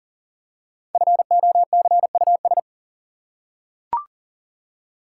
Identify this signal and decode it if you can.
Morse code, heard as a keyed tone, about 40 wpm, 700 Hz. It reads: FOCUS
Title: Five Letter Words - 40wpm